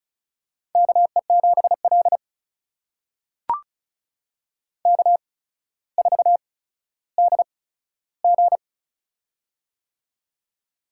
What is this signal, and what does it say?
Morse code, heard as a keyed tone, about 36 wpm, spaced as for 15 wpm, 700 Hz. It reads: KE7L K4DG
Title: USA Call Signs - 4 Characters - 8x Character Spacing - 35wpm